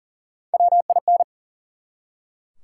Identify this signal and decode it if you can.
Morse code, heard as a keyed tone, about 37 wpm, 700 Hz. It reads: WIN
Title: Three Letter Words - 40wpm